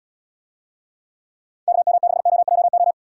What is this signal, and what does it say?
Morse code, heard as a keyed tone, about 79 wpm, 700 Hz. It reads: KM6OJO